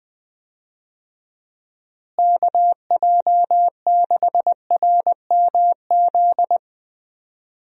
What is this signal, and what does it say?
Morse code, heard as a keyed tone, about 20 wpm, 700 Hz. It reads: KJ6RMZ
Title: USA Call Signs - 6 Characters - Repeated 3x - 20wpm